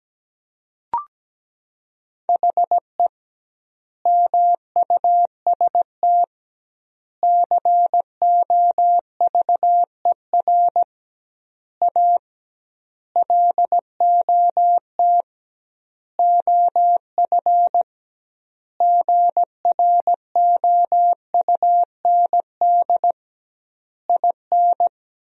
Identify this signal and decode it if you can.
Morse code, heard as a keyed tone, about 17 wpm, 700 Hz. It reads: HE MUST COVER A LOT OF GROUND IN
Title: 2x Word-Spacing - Sentences from Top 500 Words - 17wpm